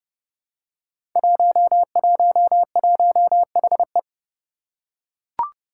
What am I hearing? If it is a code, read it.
Morse code, heard as a keyed tone, about 30 wpm, 700 Hz. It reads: 111HE
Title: Five Challenging Letter-Number Combinations 30wpm